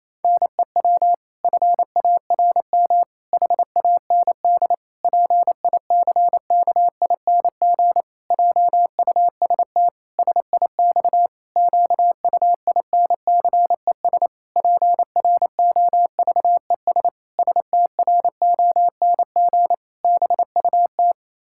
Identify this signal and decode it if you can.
Morse code, heard as a keyed tone, about 28 wpm, 700 Hz. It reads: NEW FARM HAND PICKING JUST SIX QUINCES PROVES STRONG BUT
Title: Pangram Challenge v2 - 28wpm